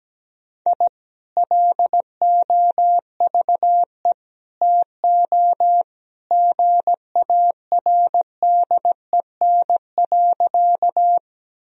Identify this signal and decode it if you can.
Morse code, heard as a keyed tone, about 17 wpm, 700 Hz. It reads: I LOVE TO GARDEN.
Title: Sentences from Top 1000 Words - Review Part 4 - 17wpm